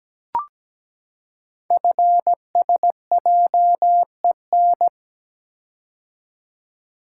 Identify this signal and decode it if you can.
Morse code, heard as a keyed tone, about 17 wpm, 700 Hz. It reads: FSJEN
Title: Code Groups - 5 Letters - 17wpm